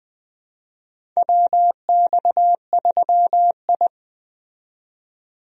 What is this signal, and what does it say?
Morse code, heard as a keyed tone, about 21 wpm, 700 Hz. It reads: WX3I
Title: USA Call Signs - 4 Characters - 20wpm